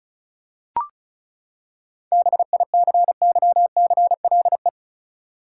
Morse code, heard as a keyed tone, about 36 wpm, 700 Hz. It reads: BICYCLE
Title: Seven Letter Words - 35wpm